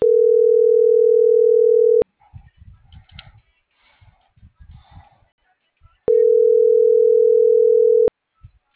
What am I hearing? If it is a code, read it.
Ringback tone (North America), 440 + 480 Hz.